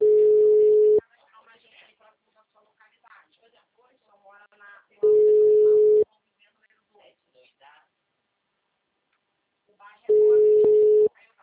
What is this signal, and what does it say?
Ringback tone (Europe, 425 Hz), with speech.